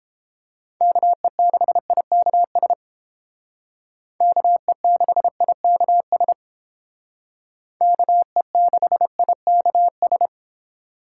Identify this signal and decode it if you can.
Morse code, heard as a keyed tone, about 29 wpm, 700 Hz. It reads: KE6IKS KE6IKS KE6IKS